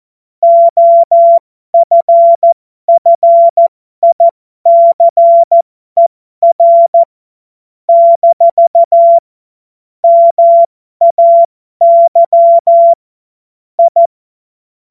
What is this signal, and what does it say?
Morse code, heard as a keyed tone, about 14 wpm, spaced as for 12 wpm, 680 Hz. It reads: OFFICER - MAY I
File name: Morsekurs / Witze